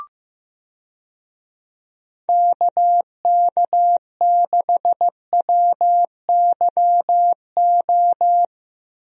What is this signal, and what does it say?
Morse code, heard as a keyed tone, about 15 wpm, 700 Hz. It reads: KK6WYO
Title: USA Call Signs - 6 Characters - Repeated 2x - 15wpm